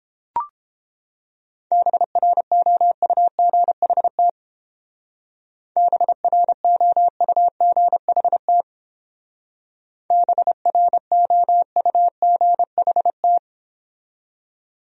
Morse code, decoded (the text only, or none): BROUGHT BROUGHT BROUGHT